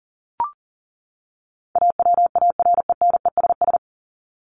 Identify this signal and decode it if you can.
Morse code, heard as a keyed tone, about 40 wpm, 700 Hz. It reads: AWARENESS
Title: Top 1501-1750 Words 40wpm